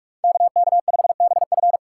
Morse code, heard as a keyed tone, about 45 wpm, 700 Hz. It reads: KK5BF
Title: USA Call Signs - 5 Characters - 3x Character Spacing - 45wpm